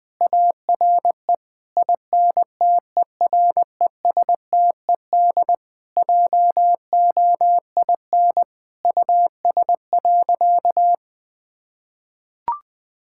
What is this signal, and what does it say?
Morse code, heard as a keyed tone, about 20 wpm, 700 Hz. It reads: ARE INTERESTED JOIN US.
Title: Sentences with 6-Word Phrases - 20wpm